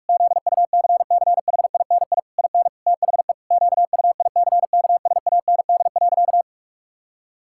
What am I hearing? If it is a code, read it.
Morse code, heard as a keyed tone, about 46 wpm, 700 Hz. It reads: ZUCCHINI IN THE QUICKSAND.
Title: Pangram Challenge v2 - 45wpm